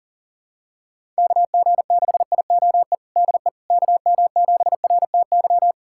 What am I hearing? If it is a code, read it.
Morse code, heard as a keyed tone, about 40 wpm, 700 Hz. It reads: KG6IOE DE KM7RTY
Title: USA Callsign DE Callsign 40wpm